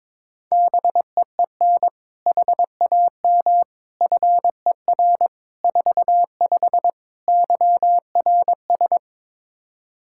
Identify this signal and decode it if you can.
Morse code, heard as a keyed tone, about 22 wpm, 700 Hz. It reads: BEEN HAM FER 45 YRS
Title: Practice QSO - Exchange 3 - More Info - 22wpm